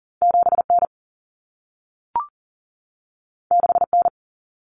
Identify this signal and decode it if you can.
Morse code, heard as a keyed tone, about 40 wpm, 700 Hz. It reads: 7N 6N
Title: Common Cut Numbers - 2-Digit 40wpm